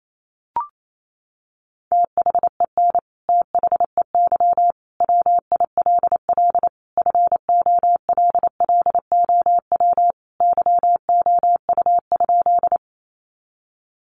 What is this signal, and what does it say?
Morse code, heard as a keyed tone, about 28 wpm, 700 Hz. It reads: THEN THEY WILL FOLLOW YOU?
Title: Sentences from Top 200 Words 28wpm